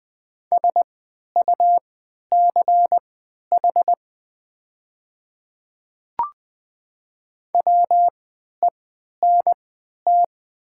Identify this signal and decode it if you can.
Morse code, heard as a keyed tone, about 20 wpm, spaced as for 11 wpm, 700 Hz. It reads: SUCH WENT
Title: 3x Character Spacing - Top 200 Words - 20wpm